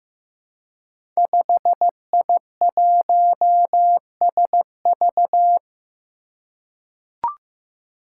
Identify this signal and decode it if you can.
Morse code, heard as a keyed tone, about 15 wpm, 700 Hz. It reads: 5I1SV
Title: Five Challenging Letter-Number Combinations 15wpm